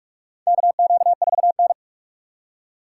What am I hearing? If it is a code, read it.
Morse code, heard as a keyed tone, about 46 wpm, 700 Hz. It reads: KQ4N
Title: USA Call Signs - 4 Characters - 8x Character Spacing - 45wpm